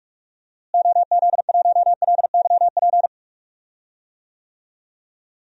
Morse code, heard as a keyed tone, about 46 wpm, 700 Hz. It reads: OZ1LYP
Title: DXCC-100 Call Signs - 6 Characters - Repeated 3x - 45wpm